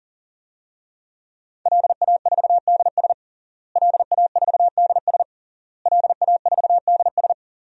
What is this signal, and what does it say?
Morse code, heard as a keyed tone, about 40 wpm, 700 Hz. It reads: LA4DS LA4DS LA4DS
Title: DXCC-100 Call Signs - 5 Characters - Repeated 3x - 40wpm